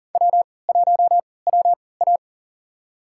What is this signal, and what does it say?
Morse code, heard as a keyed tone, about 40 wpm, spaced as for 23 wpm, 700 Hz. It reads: W1WA
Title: USA Call Signs - 4 Characters - 3x Character Spacing - 40wpm